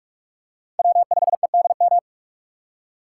Morse code, heard as a keyed tone, about 46 wpm, 700 Hz. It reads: W5EDM
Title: USA Call Signs - 5 Characters - 4x Character Spacing - 45wpm